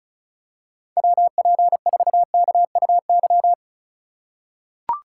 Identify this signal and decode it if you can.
Morse code, heard as a keyed tone, about 36 wpm, 700 Hz. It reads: WP4KUY
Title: USA Call Signs - 6 Characters - 2x Character Spacing - 35wpm